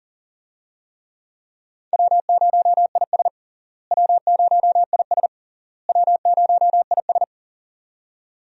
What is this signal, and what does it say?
Morse code, heard as a keyed tone, about 40 wpm, 700 Hz. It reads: W0IS W0IS W0IS
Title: USA Call Signs - 4 Characters - Repeated 3x - 40wpm